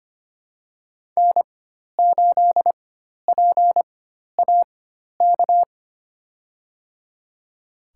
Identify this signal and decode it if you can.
Morse code, heard as a keyed tone, about 25 wpm, spaced as for 12 wpm, 700 Hz. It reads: N8PAK
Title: USA Call Signs - 4x Character Spacing - 25wpm